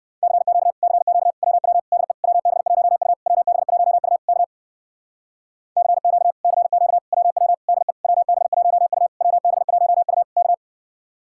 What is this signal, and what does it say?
Morse code, heard as a keyed tone, about 61 wpm, 700 Hz. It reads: CQ CQ WW DE WB1U WB1U K CQ CQ WW DE WB1U WB1U K